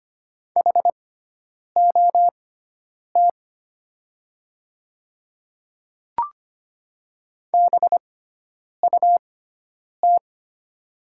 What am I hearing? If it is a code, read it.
Morse code, heard as a keyed tone, about 25 wpm, spaced as for 9 wpm, 700 Hz. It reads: HOT BUT
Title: 6x Character Spacing - Top 100 Words - 25wpm